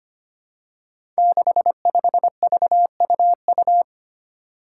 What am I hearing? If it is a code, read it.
Morse code, heard as a keyed tone, about 25 wpm, 700 Hz. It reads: 65VUU
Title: Five Challenging Letter-Number Combinations 25wpm